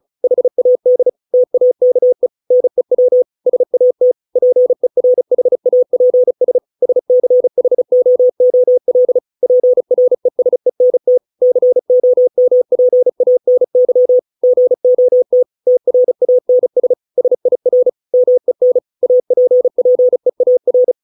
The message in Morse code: HAD TAKE NEW SAT PERHAPS SCHOOL PRESENT COMPANY GOT TRANS SIR MEN APPEAR

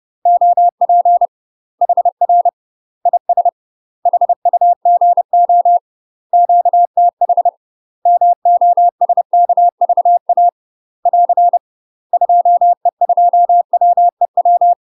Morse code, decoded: OP HR IS HUGO QTH MOSKVA <AR> 2E2WEW

